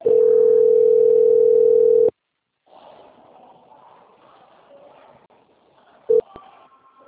Ringback tone (North America), 440 + 480 Hz.
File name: out- - - - - .9067